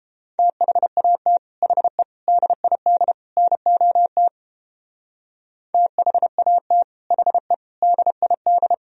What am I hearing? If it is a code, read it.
Morse code, heard as a keyed tone, about 29 wpm, 700 Hz. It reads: THAT HE DID NOT THAT HE DID